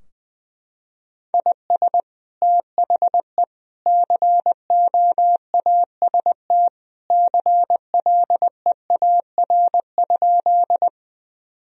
Morse code, heard as a keyed tone, about 21 wpm, 700 Hz. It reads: IS THE COAST CLEAR?
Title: Sentences from Top 1000 Words - Review Part 4 - 20wpm